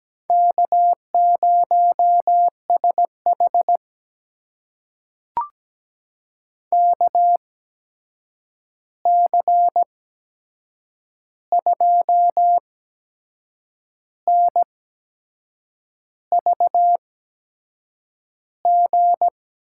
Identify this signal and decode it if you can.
Morse code, heard as a keyed tone, about 17 wpm, spaced as for 6 wpm, 700 Hz. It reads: K0SH KC2NVG
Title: USA Call Signs - 8x Character Spacing - 17wpm